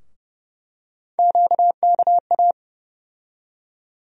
Morse code, heard as a keyed tone, about 30 wpm, 700 Hz. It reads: QKA
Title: USA Call Sign Postfixes - 3 Letters 30wpm